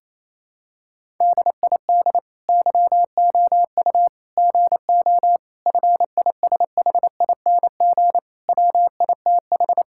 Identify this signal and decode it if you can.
Morse code, heard as a keyed tone, about 28 wpm, 700 Hz. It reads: DID YOU GO FISHING WITH
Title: Sentences from Top 1000 Words - Review Part 5 - 28wpm